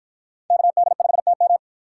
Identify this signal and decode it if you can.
Morse code, heard as a keyed tone, about 53 wpm, 700 Hz. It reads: KD5TM